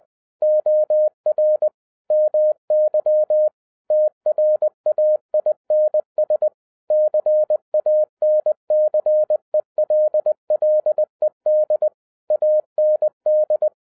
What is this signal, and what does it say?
Morse code, heard as a keyed tone, about 20 wpm, 610 Hz. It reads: OR MY TRAINS CANCELLED AND